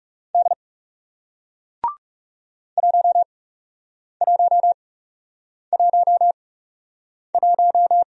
Morse code, heard as a keyed tone, about 39 wpm, spaced as for 8 wpm, 700 Hz. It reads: D 1111